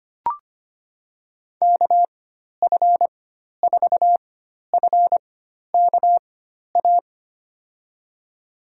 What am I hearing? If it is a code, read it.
Morse code, heard as a keyed tone, about 25 wpm, spaced as for 12 wpm, 700 Hz. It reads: KF4FKA
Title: USA Call Signs - 6 Characters - 4x Character Spacing - 25wpm